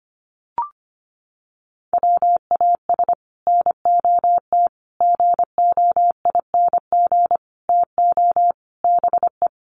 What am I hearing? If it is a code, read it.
Morse code, heard as a keyed tone, about 25 wpm, 700 Hz. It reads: WAS NOT GOING TO BE
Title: Sets of 5 Words Encore 25wpm